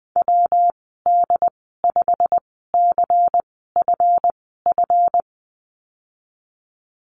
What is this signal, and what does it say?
Morse code, heard as a keyed tone, about 20 wpm, spaced as for 15 wpm, 700 Hz. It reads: WD5CFF
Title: 147 Lesson 24g - Letter y - Call Signs 20wpm